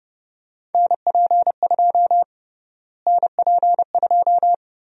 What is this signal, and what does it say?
Morse code, heard as a keyed tone, about 30 wpm, 700 Hz. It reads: NP2 NP2
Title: USA Call Sign Prefixes - Repeated 2x - 30wpm